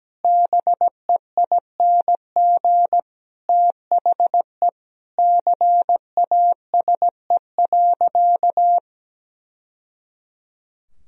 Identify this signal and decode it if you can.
Morse code, heard as a keyed tone, about 17 wpm, 700 Hz. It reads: BEING THE CASE.
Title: Sentences from Top 1000 Words - Review Part 3 - 17wpm